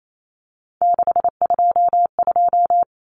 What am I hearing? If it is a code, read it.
Morse code, heard as a keyed tone, about 28 wpm, 700 Hz. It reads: 622
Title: Common Cut Numbers - 3-Digit 28wpm